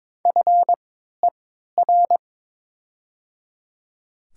Morse code, heard as a keyed tone, about 22 wpm, spaced as for 13 wpm, 700 Hz. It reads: FER
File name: Top 20 QSO Elements - 3x Character Spacing - 22wpm